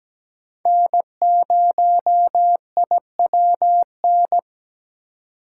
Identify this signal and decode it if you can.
Morse code, heard as a keyed tone, about 20 wpm, 700 Hz. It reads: N0IWN
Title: Speed Racing - USA Call Signs 17wpm